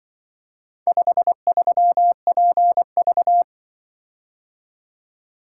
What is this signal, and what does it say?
Morse code, heard as a keyed tone, about 25 wpm, 700 Hz. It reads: 53PV